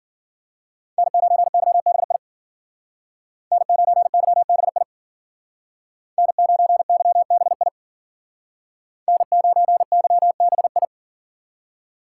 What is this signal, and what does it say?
Morse code, heard as a keyed tone, about 52 wpm, 700 Hz. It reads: N9YBI N9YBI N9YBI N9YBI